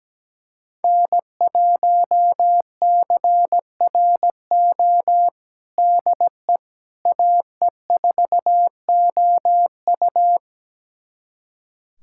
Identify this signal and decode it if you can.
Morse code, heard as a keyed tone, about 17 wpm, 700 Hz. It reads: N1CRO DE AE4OU
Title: USA Callsign DE Callsign - 5 Characters - 17wpm